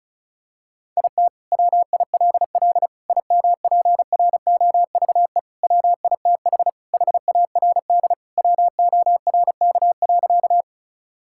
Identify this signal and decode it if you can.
Morse code, heard as a keyed tone, about 36 wpm, 700 Hz. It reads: IT WILL IMPROVE WITH HARD WORK.
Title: Sentences from Top 1000 Words - Review Part 5 - 35wpm